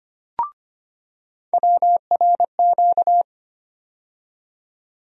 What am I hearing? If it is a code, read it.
Morse code, heard as a keyed tone, about 25 wpm, 700 Hz. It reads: WRQ
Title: USA Call Sign Postfixes - 3 Letters 25wpm